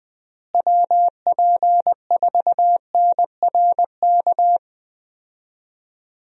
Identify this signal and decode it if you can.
Morse code, heard as a keyed tone, about 20 wpm, 700 Hz. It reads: WP4NRK
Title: USA Call Signs - 6 Characters - Repeated 3x - 20wpm